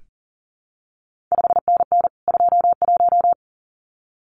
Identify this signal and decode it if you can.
Morse code, heard as a keyed tone, about 38 wpm, 700 Hz. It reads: 5NN 21
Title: CQ WW DX Contest Exchange Encore - 40wpm